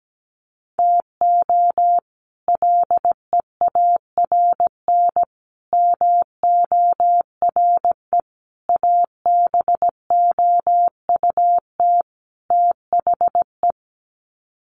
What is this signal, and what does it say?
Morse code, heard as a keyed tone, about 17 wpm, 700 Hz. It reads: TO LEARN MORE ABOUT THE
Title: Sets of 5 Words Encore2 17wpm